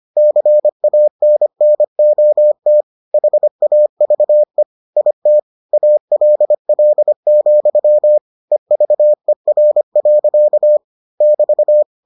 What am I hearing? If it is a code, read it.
Morse code, heard as a keyed tone, about 25 wpm, 600 Hz. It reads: CANNOT HAVE IT ALL, EVER. <BT>